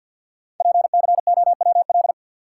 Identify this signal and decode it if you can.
Morse code, heard as a keyed tone, about 51 wpm, 690 Hz. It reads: PCOWL
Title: Code Groups - 5 Letters - 50wpm